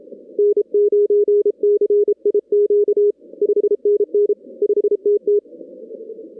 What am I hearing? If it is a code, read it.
Morse code, heard as a keyed tone, about 32 wpm, 410 Hz. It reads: N9CIQ 5NN 5TT
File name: 2016 arrl dx cw / 2016-ARRL-DX-CW-K5TR- - 0-46